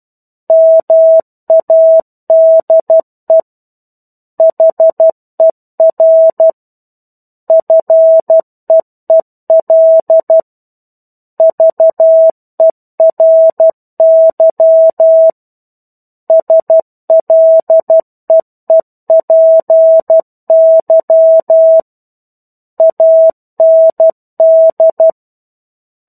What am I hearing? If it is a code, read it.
Morse code, heard as a keyed tone, about 12 wpm, 650 Hz. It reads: MADE HER FEEL VERY SLEEPY AND
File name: 01-Chapter-01 12 WPM